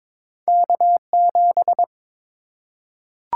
Morse code, heard as a keyed tone, about 22 wpm, 690 Hz. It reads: K7
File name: practice / 22wpm / Two Letter-Number Combinations - 22wpm